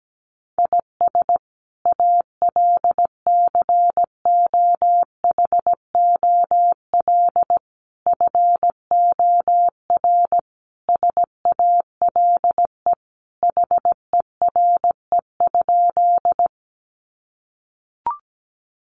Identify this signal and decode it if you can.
Morse code, heard as a keyed tone, about 17 wpm, 700 Hz. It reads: IS ALCOHOL FOR SALE HERE?
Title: Sentences from Top 1400 Words 17wpm